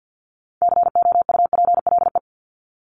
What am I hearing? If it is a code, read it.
Morse code, heard as a keyed tone, about 50 wpm, 700 Hz. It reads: COUPLE
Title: Top 1001-1250 Words 50wpm